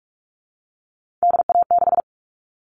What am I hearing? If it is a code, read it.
Morse code, heard as a keyed tone, about 44 wpm, 700 Hz. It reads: DA6
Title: All Cut Numbers - 3-Digit 45wpm